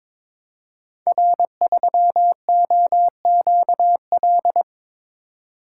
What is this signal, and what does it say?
Morse code, heard as a keyed tone, about 22 wpm, 700 Hz. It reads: R3OQL
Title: Code Groups - 5 Letter:Numbers - 22wpm